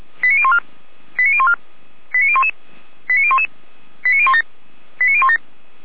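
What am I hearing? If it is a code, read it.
Five-tone selective call (ZVEI), heard as quick runs of five tones, tones of about 70 ms. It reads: 89014 89014 89011 89011 89017 89017
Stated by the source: ZVEI five-tone selective call received on SDR.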